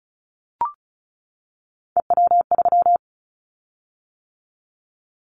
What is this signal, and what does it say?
Morse code, heard as a keyed tone, about 36 wpm, 700 Hz. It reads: EW3
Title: Callsign Prefixes for Top 50 DXCC - 35wpm